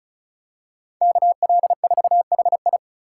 Morse code, heard as a keyed tone, about 36 wpm, 700 Hz. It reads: KL4HI
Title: USA Call Signs - 4x Character Spacing - 35wpm